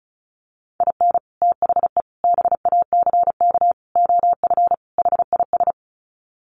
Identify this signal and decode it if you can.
Morse code, heard as a keyed tone, about 34 wpm, 700 Hz. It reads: IN THE BACK OF HIS
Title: Sets of 5 Words Encore2 35wpm